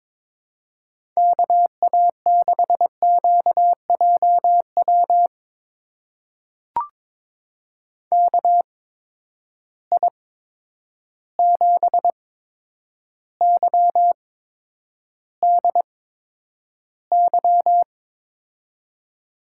Morse code, heard as a keyed tone, about 22 wpm, spaced as for 9 wpm, 700 Hz. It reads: KA6QJW KI7YDY
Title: USA Call Signs - 6 Characters - 8x Character Spacing - 22wpm